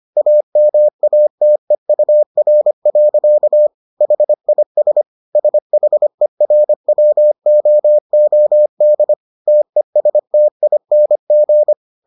Morse code, heard as a keyed tone, about 25 wpm, 600 Hz. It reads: AMATEUR. HIS SHERWOOD TESTING